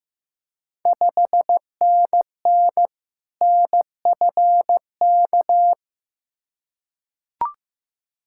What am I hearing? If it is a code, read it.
Morse code, heard as a keyed tone, about 15 wpm, 700 Hz. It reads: 5NN NFK
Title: Ontario QSO Party Exchange 15wpm